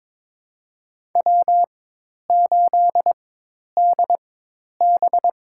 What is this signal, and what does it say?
Morse code, heard as a keyed tone, about 22 wpm, spaced as for 10 wpm, 700 Hz. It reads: W8DB
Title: USA Call Signs - 4 Characters - 4x Character Spacing - 22wpm